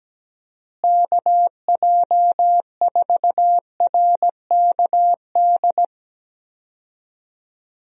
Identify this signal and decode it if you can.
Morse code, heard as a keyed tone, about 17 wpm, 700 Hz. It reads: KJ4RKD
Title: USA Call Signs - 6 Characters - Repeated 3x - 17wpm